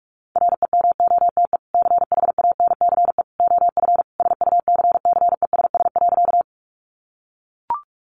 Morse code, heard as a keyed tone, about 44 wpm, 700 Hz. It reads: REMOTE CHANCE OF SUCCESS.
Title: Sentences from Top 1300 Words 45wpm